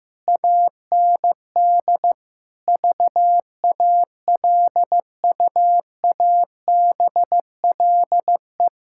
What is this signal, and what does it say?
Morse code, heard as a keyed tone, about 15 wpm, 700 Hz. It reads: AND VALUABLE
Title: Pangram Challenge v2 - 15wpm